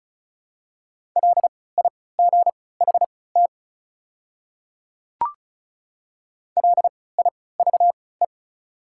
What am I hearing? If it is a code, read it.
Morse code, heard as a keyed tone, about 36 wpm, spaced as for 20 wpm, 700 Hz. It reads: LIGHT LIVE
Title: 3x Character Spacing - Top 200 Words - 35wpm